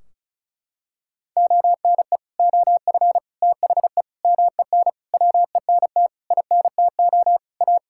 Morse code, heard as a keyed tone, about 36 wpm, 700 Hz. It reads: ONE OF THE MEN WENT INTO A
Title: Sentences from Top 1600 Words 35wpm